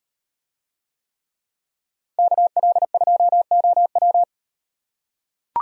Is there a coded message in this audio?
Yes: Morse code from 2.2 to 4.2 s, heard as a keyed tone, about 38 wpm, 700 Hz. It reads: KL2OW